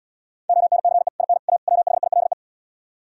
Morse code, heard as a keyed tone, about 77 wpm, 700 Hz. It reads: ONCE IN A WHILE